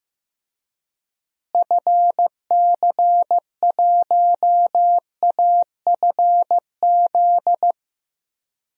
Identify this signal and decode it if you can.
Morse code, heard as a keyed tone, about 15 wpm, 700 Hz. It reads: FC1AFZ
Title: DXCC-100 Call Signs - 6 Characters - Repeated 2x - 15wpm